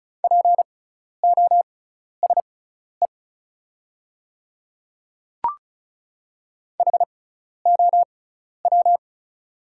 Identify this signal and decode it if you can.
Morse code, heard as a keyed tone, about 36 wpm, spaced as for 12 wpm, 690 Hz. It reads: POSE HOW
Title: 6x Character Spacing - Top 500 Words - 35wpm